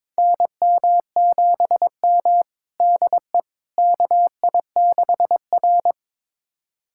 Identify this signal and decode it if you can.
Morse code, heard as a keyed tone, about 22 wpm, 700 Hz. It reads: NM7M DE KI6R
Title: USA Callsign DE Callsign - 4 Characters - 22wpm